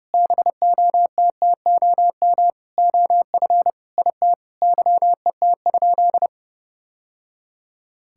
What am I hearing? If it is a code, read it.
Morse code, heard as a keyed tone, about 30 wpm, 700 Hz. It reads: BOTTOM OF IT YET?
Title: Sentences from Top 1000 Words - Review Part 6 - 30wpm